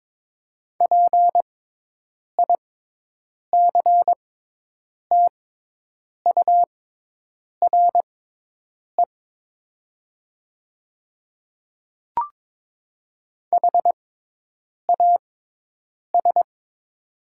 Morse code, heard as a keyed tone, about 22 wpm, spaced as for 8 wpm, 700 Hz. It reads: PICTURE HAS
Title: 6x Character Spacing - Top 200 Words - 22wpm